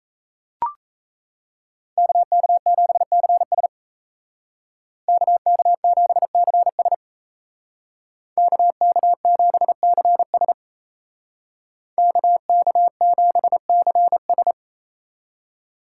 Morse code, heard as a keyed tone, about 37 wpm, 700 Hz. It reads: KK7CS KK7CS KK7CS KK7CS